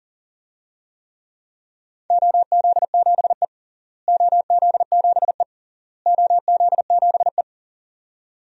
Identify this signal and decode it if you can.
Morse code, heard as a keyed tone, about 37 wpm, 700 Hz. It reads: OZ7E OZ7E OZ7E
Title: DXCC-100 Call Signs - 4 Characters - Repeated 3x - 40wpm